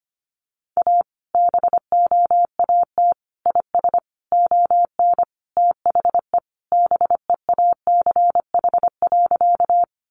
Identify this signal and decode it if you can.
Morse code, heard as a keyed tone, about 25 wpm, 700 Hz. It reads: A BOAT IS ON THE BEACH.
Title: Sentences from Top 1500 Words 25wpm